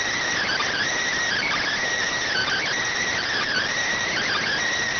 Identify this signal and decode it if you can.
Emergency Alert System (SAME) tones, heard as harsh buzzing bursts. It is heard from 0.4 to 4.6 s.